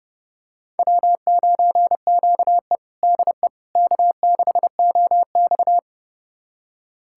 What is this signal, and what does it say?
Morse code, heard as a keyed tone, about 30 wpm, 700 Hz. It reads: W9QE DE K6OX
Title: USA Callsign DE Callsign - 4 Characters - 30wpm